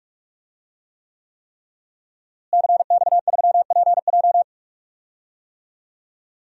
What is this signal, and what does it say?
Morse code, heard as a keyed tone, about 46 wpm, 700 Hz. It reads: CX3PJ
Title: DXCC-100 Call Signs - 5 Characters - Repeated 3x - 45wpm